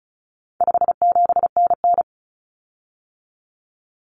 Morse code, heard as a keyed tone, about 34 wpm, 700 Hz. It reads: 57NN
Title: Common Cut Numbers - 4-Digit 35wpm